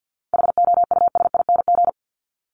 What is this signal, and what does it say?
Morse code, heard as a keyed tone, about 50 wpm, 700 Hz. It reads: HOUSING
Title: ING Suffix - Top 100 Words with Base Words 50wpm